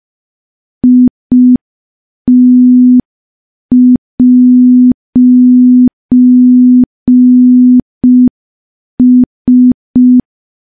Morse code, heard as a keyed tone, about 5 wpm, 250 Hz. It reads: IT'S
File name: morse / morse-training-05WPM